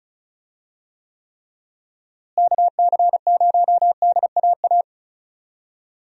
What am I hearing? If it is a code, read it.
Morse code, heard as a keyed tone, about 34 wpm, 700 Hz. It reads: KC0DAA
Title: USA Call Signs - 6 Characters - Repeated 3x - 35wpm